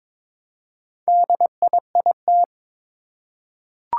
Morse code, heard as a keyed tone, about 23 wpm, 700 Hz. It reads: DIIT
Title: Four Challenging Letter-Number Combinations 22wpm